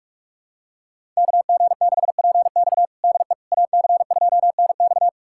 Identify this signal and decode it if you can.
Morse code, heard as a keyed tone, about 43 wpm, 700 Hz. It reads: KG6PX DE AC2NX
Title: USA Callsign DE Callsign - 5 Characters - 45wpm